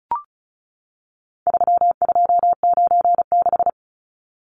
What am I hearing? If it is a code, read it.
Morse code, heard as a keyed tone, about 34 wpm, 700 Hz. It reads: 3296
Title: Numbers - 4-Digit 35wpm